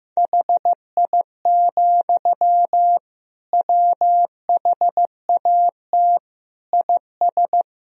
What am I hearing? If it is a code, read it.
Morse code, heard as a keyed tone, about 15 wpm, 700 Hz. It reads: HI, WHAT IS
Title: Sentences from Top 1000 Words - Review Part 3 - 15wpm